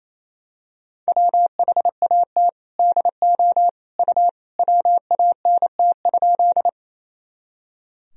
Morse code, heard as a keyed tone, about 28 wpm, 700 Hz. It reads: WHAT DO U WANT?
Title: Sentences from Top 200 Words and Abbreviations 28wpm